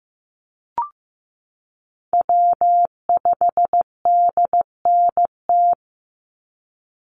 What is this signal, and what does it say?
Morse code, heard as a keyed tone, about 15 wpm, 700 Hz. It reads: W5DNT
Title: Call Signs Repeated 2x - 15wpm